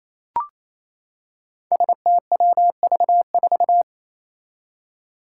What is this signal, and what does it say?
Morse code, heard as a keyed tone, about 29 wpm, 700 Hz. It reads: STWV4
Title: Five Challenging Letter-Number Combinations 28wpm